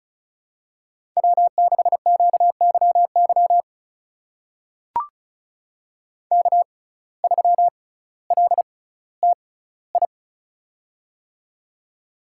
Morse code, heard as a keyed tone, about 36 wpm, spaced as for 18 wpm, 700 Hz. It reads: W6QYY K3LTI
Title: USA Call Signs - 5 Characters - 6x Character Spacing - 35wpm